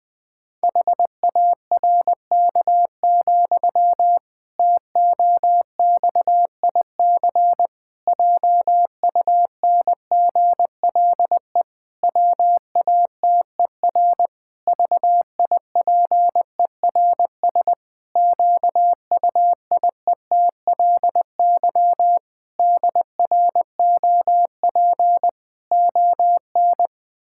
Morse code, heard as a keyed tone, about 20 wpm, 700 Hz. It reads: HARK, TOXIC JUNGLE WATER VIPERS QUIETLY DROP ON
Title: Pangram Challenge v2 - 20wpm